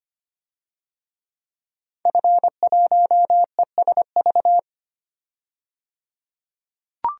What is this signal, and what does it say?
Morse code, heard as a keyed tone, about 25 wpm, 700 Hz. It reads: F1ESV
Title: DXCC-100 Call Signs - 5 Characters - 25wpm